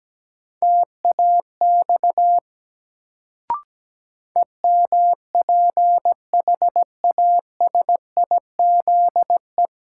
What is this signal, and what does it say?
Morse code, heard as a keyed tone, about 17 wpm, 700 Hz. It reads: TAX EMPHASIZE
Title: Sentences from Top 1500 Words - New Word Drill Alt - 17wpm